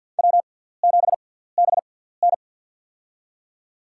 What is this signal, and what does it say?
Morse code, heard as a keyed tone, about 50 wpm, spaced as for 17 wpm, 700 Hz. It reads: W7BN
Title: USA Call Signs - 4 Characters - 6x Character Spacing - 50wpm